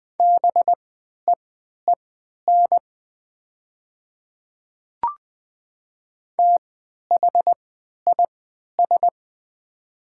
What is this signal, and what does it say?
Morse code, heard as a keyed tone, about 20 wpm, spaced as for 11 wpm, 700 Hz. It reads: BEEN THIS
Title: 3x Character Spacing - Top 500 Words - 20wpm